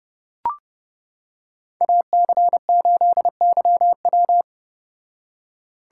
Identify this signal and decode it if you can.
Morse code, heard as a keyed tone, about 29 wpm, 700 Hz. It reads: AC8YW